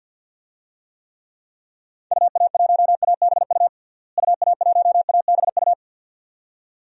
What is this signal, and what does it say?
Morse code, heard as a keyed tone, about 46 wpm, 700 Hz. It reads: UA1ABU UA1ABU
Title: DXCC-100 Call Signs - 6 Characters - Repeated 2x - 50wpm